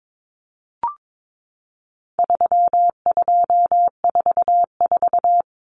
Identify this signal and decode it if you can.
Morse code, heard as a keyed tone, about 22 wpm, 700 Hz. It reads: 3244
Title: Common Cut Numbers - 4-Digit 22wpm